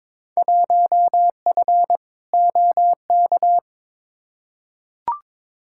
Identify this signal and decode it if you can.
Morse code, heard as a keyed tone, about 22 wpm, 700 Hz. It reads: 1F OK
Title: Field Day Exchanges Encore - 22wpm